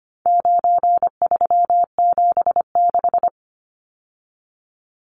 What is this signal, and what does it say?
Morse code, heard as a keyed tone, about 25 wpm, 700 Hz. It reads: 9376
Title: Numbers - 4-Digit 25wpm